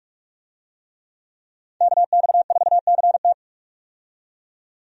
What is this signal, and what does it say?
Morse code, heard as a keyed tone, about 45 wpm, 700 Hz. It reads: KX4CT
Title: USA Call Signs - 5 Characters - 45wpm